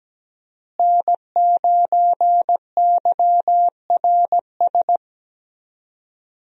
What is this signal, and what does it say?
Morse code, heard as a keyed tone, about 17 wpm, 700 Hz. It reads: N9YRS
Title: USA Call Signs - 5 Characters - Repeated 3x - 17wpm